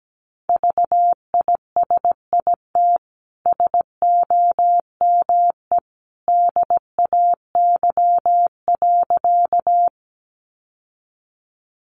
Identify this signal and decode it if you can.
Morse code, heard as a keyed tone, about 17 wpm, 700 Hz. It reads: VISIT SOME DAY.
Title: Sentences from Top 900 Words 17wpm